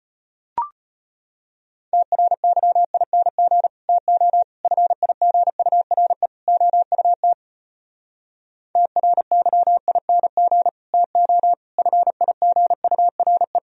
Morse code, heard as a keyed tone, about 33 wpm, 700 Hz. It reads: TRYING TO FIGURE OUT TRYING TO FIGURE